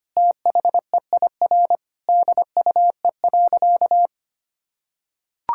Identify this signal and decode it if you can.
Morse code, heard as a keyed tone, about 25 wpm, 700 Hz. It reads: THEIR DUE.
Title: Sentences from Top 1000 Words - Review Part 6 - 25wpm